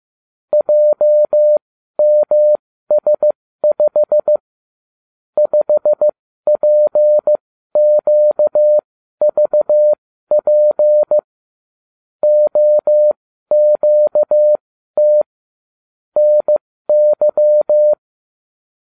Morse code, heard as a keyed tone, about 15 wpm, spaced as for 12 wpm, 600 Hz. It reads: JMS5 5PQVP OQT NY